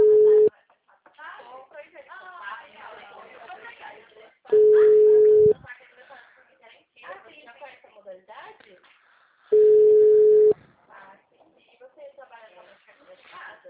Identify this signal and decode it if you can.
Ringback tone (Europe, 425 Hz), with speech.